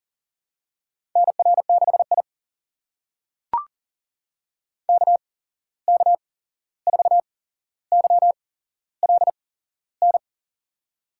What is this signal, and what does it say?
Morse code, heard as a keyed tone, about 40 wpm, spaced as for 15 wpm, 700 Hz. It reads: NR6I KK4YLN
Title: USA Call Signs - 8x Character Spacing - 40wpm